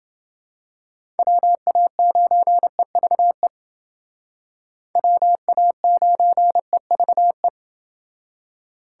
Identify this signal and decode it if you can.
Morse code, heard as a keyed tone, about 28 wpm, 700 Hz. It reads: WA9EVE WA9EVE